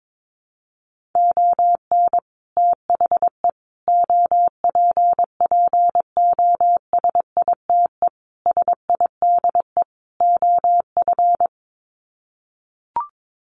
Morse code, heard as a keyed tone, about 22 wpm, 700 Hz. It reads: ON THE OPPOSITE SIDE OF
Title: Sets of 5 Words Encore2 22wpm